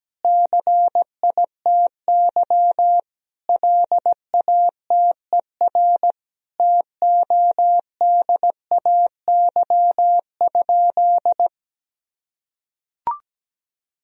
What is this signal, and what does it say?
Morse code, heard as a keyed tone, about 17 wpm, 700 Hz. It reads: CITY LATER TODAY?
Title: Sentences from Top 1000 Words - Review Part 2 - 17wpm